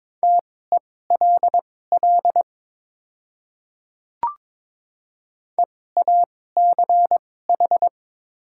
Morse code, heard as a keyed tone, about 22 wpm, spaced as for 16 wpm, 700 Hz. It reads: TELL EACH
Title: 2x Character Spacing - Top 200 Words - 22wpm